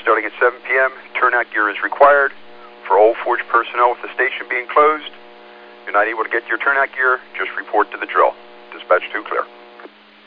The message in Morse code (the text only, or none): none